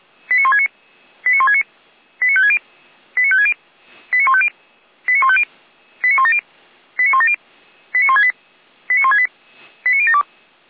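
Five-tone selective call (ZVEI), heard as quick runs of five tones, tones of about 70 ms. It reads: 89169 89169 89560 89560 89150 89150 89170 89170 89168 89168 89072
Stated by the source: ZVEI five-tone call, address 89223, received on an SDR by feuerwehr-orlamuende (Orlamünde fire brigade).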